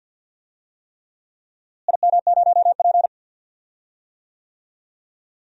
Morse code, heard as a keyed tone, about 53 wpm, 700 Hz. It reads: IM0P
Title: DXCC-100 Call Signs - 4 Characters - Repeated 3x - 50wpm